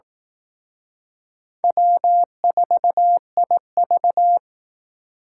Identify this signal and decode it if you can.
Morse code, heard as a keyed tone, about 18 wpm, 700 Hz. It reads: W4IV
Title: Speed Racing - USA 1x2 Call Signs 15wpm